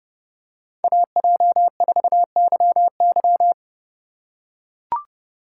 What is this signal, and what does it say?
Morse code, heard as a keyed tone, about 30 wpm, 700 Hz. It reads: AJ4YY
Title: USA Call Signs - 6x Character Spacing - 30wpm